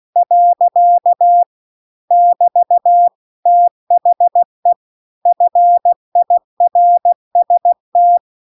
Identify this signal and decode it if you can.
Morse code, heard as a keyed tone, about 16 wpm, 700 Hz. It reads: . <BT> THE FIRST
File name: CW-The-Big-Twoheaded-River-2-16WPM